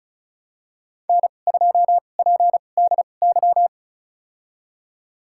Morse code, heard as a keyed tone, about 34 wpm, spaced as for 25 wpm, 700 Hz. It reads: N2PDY
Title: USA Call Signs - 5 Characters - 2x Character Spacing - 35wpm